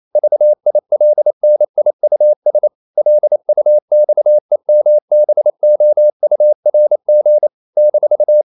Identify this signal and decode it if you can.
Morse code, heard as a keyed tone, about 28 wpm, 600 Hz. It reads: VILNIUS LUXEMBOURG -
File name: CW / Practice / NationalCapitals28